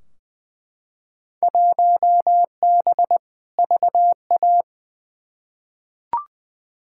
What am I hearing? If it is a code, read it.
Morse code, heard as a keyed tone, about 20 wpm, 700 Hz. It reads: 1B VA
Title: Field Day Exchanges - 20wpm